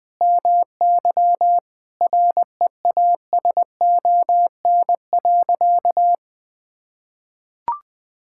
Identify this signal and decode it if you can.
Morse code, heard as a keyed tone, about 20 wpm, 700 Hz. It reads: MY REASON.
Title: Sentences from Top 1000 Words - Review Part 5 - 20wpm